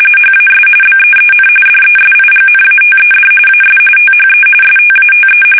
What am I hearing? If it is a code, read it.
RTTY, about 45.45 baud. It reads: HAS BEEN DESIGNATED AS THE STATE
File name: fsk850 / battleshipnj